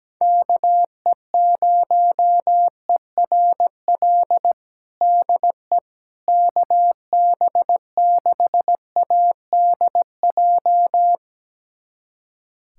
Morse code, heard as a keyed tone, about 17 wpm, 700 Hz. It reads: KE0ERL DE KB6ADJ
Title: USA Callsign DE Callsign 17wpm